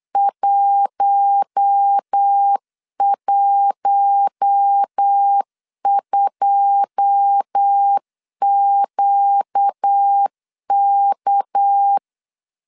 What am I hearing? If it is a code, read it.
Morse code, heard as a keyed tone, about 8 wpm, 800 Hz. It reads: 112QK
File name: LEZIONI / 9 2 8wpm 3437